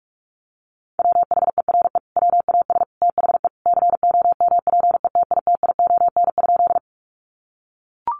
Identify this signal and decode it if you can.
Morse code, heard as a keyed tone, about 44 wpm, 700 Hz. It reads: WHERE WAS THE COMPETITION?
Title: Sentences from Top 1000 Words 45wpm